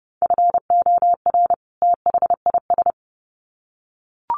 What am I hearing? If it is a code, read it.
Morse code, heard as a keyed tone, about 30 wpm, 700 Hz. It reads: FOR THIS
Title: Sets of 2 Words 30wpm